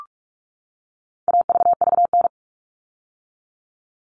Morse code, heard as a keyed tone, about 44 wpm, 700 Hz. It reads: AVVN
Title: All Cut Numbers - 4-Digit 45wpm